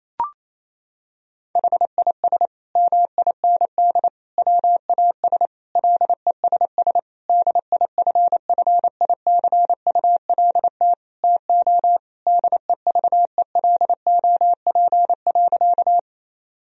Morse code, heard as a keyed tone, about 28 wpm, 700 Hz. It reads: HIS MIND WAS LESS DIFFICULT TO DEVELOP.
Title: Sentences from Top 1000 Words - Review Part 2 - 28wpm